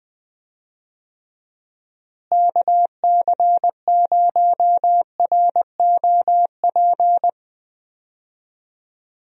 Morse code, heard as a keyed tone, about 20 wpm, 700 Hz. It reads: KC0ROP
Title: USA Call Signs - 6 Characters - Repeated 2x - 20wpm